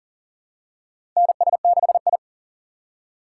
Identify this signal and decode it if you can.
Morse code, heard as a keyed tone, about 40 wpm, 700 Hz. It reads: NS6I
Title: USA Call Signs - 6x Character Spacing - 40wpm